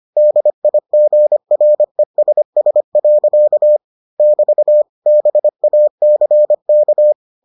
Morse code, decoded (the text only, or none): DIGRESS. <BT> BACK